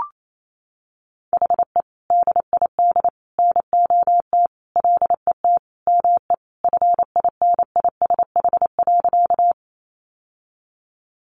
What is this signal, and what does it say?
Morse code, heard as a keyed tone, about 28 wpm, 700 Hz. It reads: HE DID NOT LET ME FINISH.
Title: Sentences from Top 1200 Words 28wpm